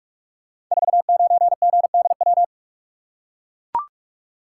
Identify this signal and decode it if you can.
Morse code, heard as a keyed tone, about 44 wpm, 700 Hz. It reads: 49GDW
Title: Code Groups - 5 Letter:Numbers - 45wpm